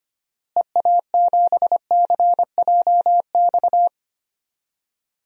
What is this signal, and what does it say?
Morse code, heard as a keyed tone, about 25 wpm, 700 Hz. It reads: EA7CJX
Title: DXCC-100 Call Signs - 6 Characters - Repeated 3x - 25wpm